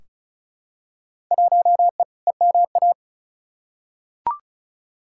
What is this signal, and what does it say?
Morse code, heard as a keyed tone, about 36 wpm, 700 Hz. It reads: 1E EMA
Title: Field Day Exchanges Encore - 35wpm